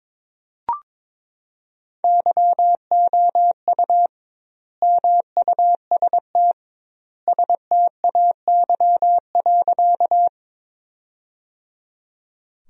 Morse code, heard as a keyed tone, about 22 wpm, 700 Hz. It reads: YOU MUST STAY.
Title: 2x Word-Spacing - Sentences from Top 500 Words - 22wpm